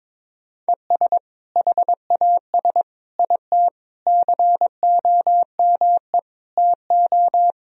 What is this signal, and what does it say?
Morse code, heard as a keyed tone, about 22 wpm, 700 Hz. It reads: ES HAS IT COME TO
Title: Sentences from Top 100 Words and Abbreviations 22wpm